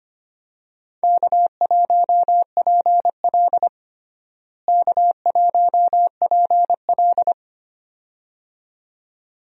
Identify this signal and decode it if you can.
Morse code, heard as a keyed tone, about 25 wpm, 700 Hz. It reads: K1PL K1PL
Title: USA Call Signs - 4 Characters - Repeated 2x - 25wpm